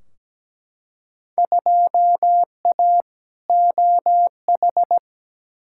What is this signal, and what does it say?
Morse code, heard as a keyed tone, about 17 wpm, 700 Hz. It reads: 2A OH
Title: Field Day Exchanges - 17wpm